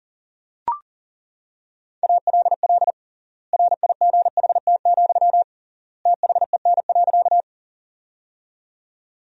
Morse code, heard as a keyed tone, about 40 wpm, 700 Hz. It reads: ALL RIGHT, THEN.
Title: 3x Word-Spacing - Sentences from Top 200 Words - 40wpm